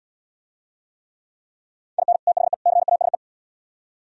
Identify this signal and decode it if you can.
Morse code, heard as a keyed tone, about 69 wpm, 700 Hz. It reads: IN THE CASE